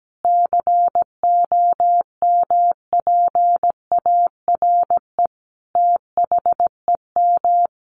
Morse code, heard as a keyed tone, about 17 wpm, 700 Hz. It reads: COMPARE THEM
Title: Sentences from Top 1300 Words 17wpm